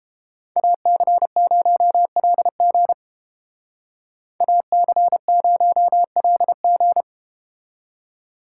Morse code, heard as a keyed tone, about 30 wpm, 700 Hz. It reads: AC0LG AC0LG